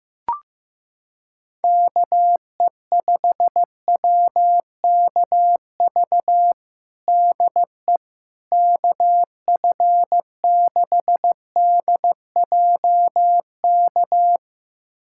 Morse code, heard as a keyed tone, about 15 wpm, 700 Hz. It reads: KE5WKV DE KF6DJK
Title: USA Callsign DE Callsign - 6 Characters - 15wpm